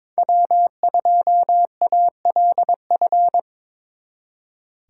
Morse code, heard as a keyed tone, about 25 wpm, 700 Hz. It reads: W2ALF
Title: Speed Racing - USA 1x3 Call Signs 22wpm